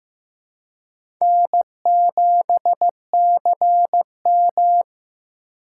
Morse code, heard as a keyed tone, about 15 wpm, 700 Hz. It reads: N7CM
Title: USA Call Signs - 4 Characters - 6x Character Spacing - 15wpm